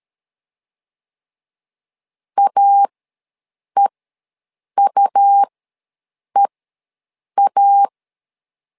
Morse code, heard as a keyed tone, about 13 wpm, spaced as for 7 wpm, 800 Hz. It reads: AEUEA
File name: LEZIONI / 2 UHDN / 2 U ws 12wpm 1655